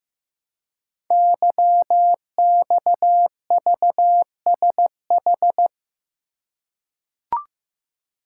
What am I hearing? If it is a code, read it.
Morse code, heard as a keyed tone, about 15 wpm, 700 Hz. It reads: YXVSH
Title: Code Groups - 5 Letter:Numbers - 15wpm